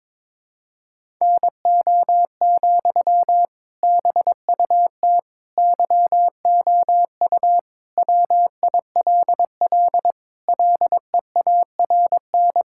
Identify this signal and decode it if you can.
Morse code, heard as a keyed tone, about 22 wpm, 700 Hz. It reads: NO, BUT YOU WILL LEARN
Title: Sentences from Top 1000 Words - Review Part 6 - 22wpm